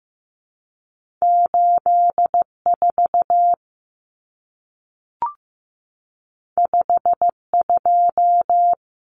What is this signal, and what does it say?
Morse code, heard as a keyed tone, about 15 wpm, 700 Hz. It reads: 84 52
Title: Numbers - 2-Digit 15wpm